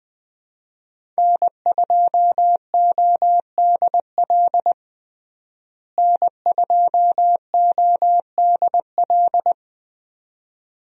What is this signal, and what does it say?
Morse code, heard as a keyed tone, about 20 wpm, 700 Hz. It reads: N2ODL N2ODL
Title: USA Call Signs - 5 Characters - Repeated 2x - 20wpm